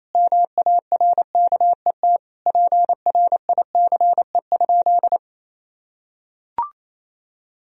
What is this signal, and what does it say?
Morse code, heard as a keyed tone, about 28 wpm, 700 Hz. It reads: MARKET PRICE?
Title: Sentences from Top 1000 Words - Review Part 3 - 28wpm